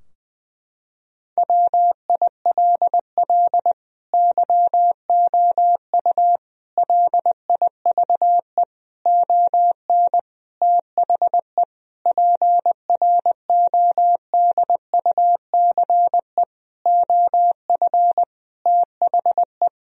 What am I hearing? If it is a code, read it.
Morse code, heard as a keyed tone, about 20 wpm, 700 Hz. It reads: WILL YOU LIVE ON THE PRODUCE OF THE
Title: Sentences from Top 1000 Words - Review Part 3 - 20wpm